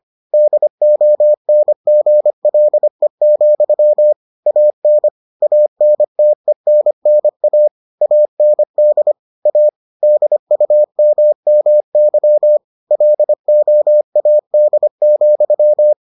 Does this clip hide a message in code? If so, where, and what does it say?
Yes: Morse code from 0.3 to 15.9 s, heard as a keyed tone, about 25 wpm, 600 Hz. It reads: DONGLE, AN ANTENNA AND A DUMMY LOAD,